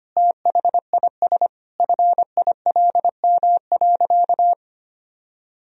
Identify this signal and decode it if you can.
Morse code, heard as a keyed tone, about 25 wpm, 700 Hz. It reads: THIS FILM.
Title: Sentences from Top 1000 Words - Review Part 3 - 25wpm